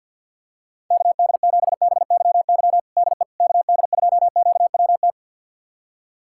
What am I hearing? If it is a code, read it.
Morse code, heard as a keyed tone, about 46 wpm, 700 Hz. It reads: KD7BYY DE KD2QWT
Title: USA Callsign DE Callsign - 6 Characters - 50wpm